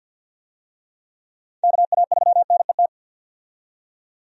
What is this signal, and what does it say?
Morse code, heard as a keyed tone, about 50 wpm, 700 Hz. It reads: KA3NET
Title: USA Call Signs - 6 Characters - 50wpm